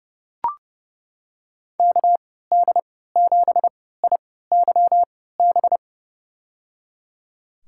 Morse code, heard as a keyed tone, about 30 wpm, spaced as for 17 wpm, 700 Hz. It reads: KD7IYB
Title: USA Call Signs - 6 Characters - 3x Character Spacing - 30wpm